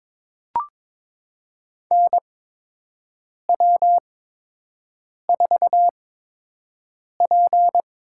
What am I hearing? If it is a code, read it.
Morse code, heard as a keyed tone, about 22 wpm, spaced as for 6 wpm, 700 Hz. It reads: NW4P
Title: USA Call Signs - 4 Characters - 8x Character Spacing - 22wpm